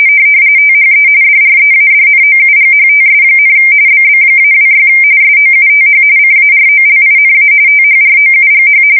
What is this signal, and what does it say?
RTTY, about 45.45 baud. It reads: THE DREADNOUGHTS, BB-26 SOUTH CAROLINA THROUGH
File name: fsk170 / battleships